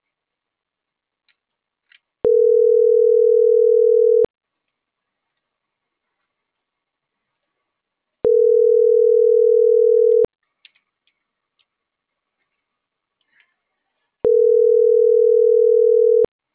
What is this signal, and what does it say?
Ringback tone (North America), 440 + 480 Hz.